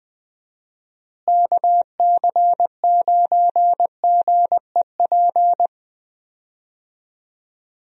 Morse code, heard as a keyed tone, about 20 wpm, 700 Hz. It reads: KC9GEP
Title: USA Call Signs - 6 Characters - Repeated 2x - 20wpm